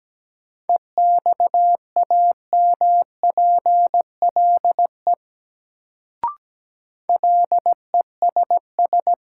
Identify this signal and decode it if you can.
Morse code, heard as a keyed tone, about 17 wpm, 700 Hz. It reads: EXAMPLE LESS
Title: Sentences from Top 400 Words - New Word Drill - 17wpm